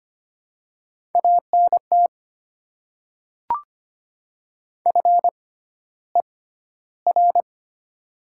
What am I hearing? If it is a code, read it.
Morse code, heard as a keyed tone, about 25 wpm, spaced as for 13 wpm, 700 Hz. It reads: ANT FER
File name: Top 20 QSO Elements - 6x Character Spacing - 25wpm